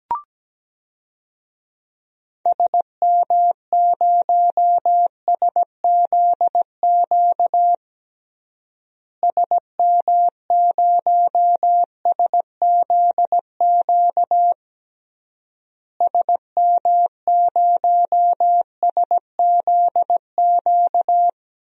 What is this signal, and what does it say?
Morse code, heard as a keyed tone, about 17 wpm, 700 Hz. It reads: SM0SZQ SM0SZQ SM0SZQ
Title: DXCC-100 Call Signs - 6 Characters - Repeated 3x - 17wpm